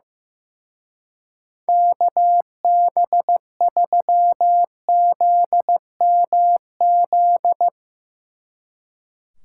Morse code, heard as a keyed tone, about 16 wpm, 700 Hz. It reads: KB3ZMZ
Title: Speed Racing - USA 2x3 Call Signs 15wpm